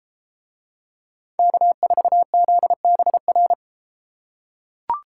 Morse code, heard as a keyed tone, about 32 wpm, 700 Hz. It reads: K4ZBR